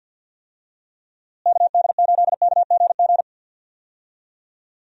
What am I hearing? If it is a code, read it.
Morse code, heard as a keyed tone, about 52 wpm, 700 Hz. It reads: KD8KGG
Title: USA Call Signs - 6 Characters - Repeated 3x - 50wpm